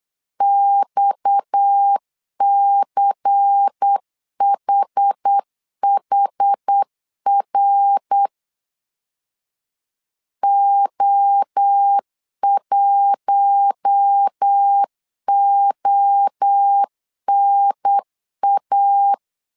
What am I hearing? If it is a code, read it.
Morse code, heard as a keyed tone, about 8 wpm, 800 Hz. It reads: XCHHR O1ONA
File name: 8wpm